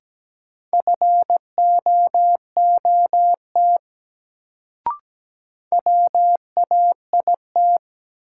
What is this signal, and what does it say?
Morse code, heard as a keyed tone, about 17 wpm, 700 Hz. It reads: FOOT WAIT
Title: Sentences from Top 500 Words - New Word Drill - 17wpm